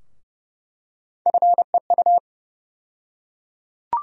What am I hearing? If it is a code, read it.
Morse code, heard as a keyed tone, about 30 wpm, 700 Hz. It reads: FEU
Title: USA Call Sign Postfixes - 3 Letters 30wpm